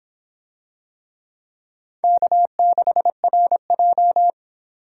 Morse code, heard as a keyed tone, about 26 wpm, 700 Hz. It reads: K6RJ